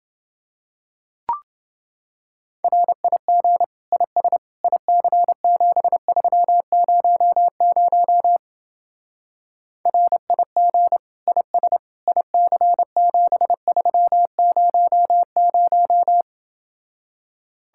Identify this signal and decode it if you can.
Morse code, heard as a keyed tone, about 29 wpm, 700 Hz. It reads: RIG IS IC7300 RIG IS IC7300